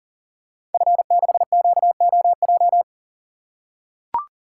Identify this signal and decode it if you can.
Morse code, heard as a keyed tone, about 38 wpm, 690 Hz. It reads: F6QOJ
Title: Code Groups - 5 Letter:Numbers - 40wpm